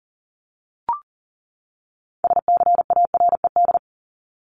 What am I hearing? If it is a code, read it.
Morse code, heard as a keyed tone, about 40 wpm, 700 Hz. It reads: SCARED
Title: Top 501-750 Words 40wpm